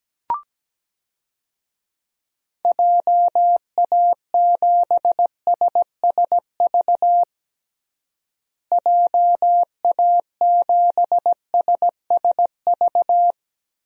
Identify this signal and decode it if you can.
Morse code, heard as a keyed tone, about 17 wpm, 700 Hz. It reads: JA7SSV JA7SSV
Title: DXCC-100 Call Signs - 6 Characters - Repeated 3x - 17wpm